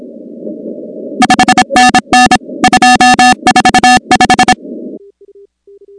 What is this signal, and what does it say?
Morse code, heard as a keyed tone, about 26 wpm, 750 Hz, with noise. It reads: 5NN 245
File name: contests / 2019-cw-wpx-kt5j-k5pi- -2330 0-16